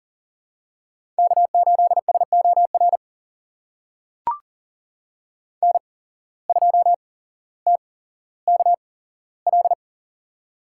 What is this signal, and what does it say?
Morse code, heard as a keyed tone, about 40 wpm, spaced as for 17 wpm, 700 Hz. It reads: K8SOR N2TKL
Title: USA Call Signs - 5 Characters - 8x Character Spacing - 40wpm